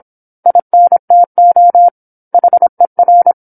Morse code, heard as a keyed tone, about 27 wpm, 700 Hz. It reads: INTO HER